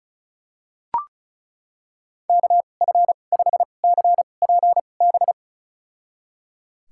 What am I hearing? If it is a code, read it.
Morse code, heard as a keyed tone, about 36 wpm, spaced as for 25 wpm, 700 Hz. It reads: KF5CPB
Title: USA Call Signs - 6 Characters - 2x Character Spacing - 35wpm